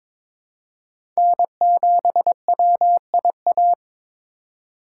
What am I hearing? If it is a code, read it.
Morse code, heard as a keyed tone, about 22 wpm, 700 Hz. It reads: N7WIA
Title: USA Call Signs - 5 Characters - 6x Character Spacing - 22wpm